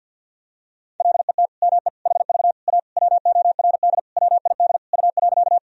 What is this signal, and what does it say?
Morse code, heard as a keyed tone, about 52 wpm, 700 Hz. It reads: LET ME HV A WORD WID U.
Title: Sentences from Top 300 Words and Abbreviations 50wpm